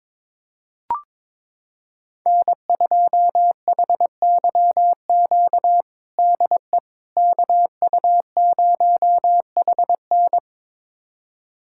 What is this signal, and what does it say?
Morse code, heard as a keyed tone, about 22 wpm, 700 Hz. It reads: N2HYQ DE KU0HN
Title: USA Callsign DE Callsign - 5 Characters - 22wpm